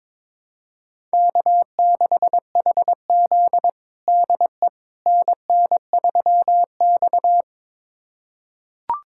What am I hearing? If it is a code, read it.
Morse code, heard as a keyed tone, about 21 wpm, 700 Hz. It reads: K6HZ DE NN3X
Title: USA Callsign DE Callsign - 4 Characters - 22wpm